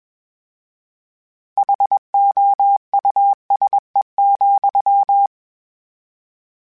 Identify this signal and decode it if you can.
Morse code, heard as a keyed tone, about 21 wpm, 800 Hz. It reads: HOUSE,
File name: General / General-mp3s / Houses @20